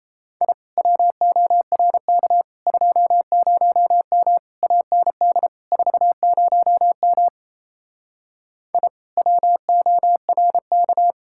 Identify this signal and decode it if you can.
Morse code, heard as a keyed tone, about 37 wpm, 700 Hz. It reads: I WORK 20M AND 40M I WORK